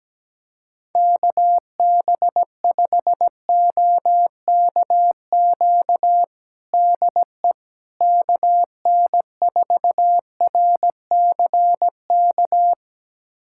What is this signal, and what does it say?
Morse code, heard as a keyed tone, about 17 wpm, 700 Hz. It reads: KB5OKQ DE KN4RCK
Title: USA Callsign DE Callsign 17wpm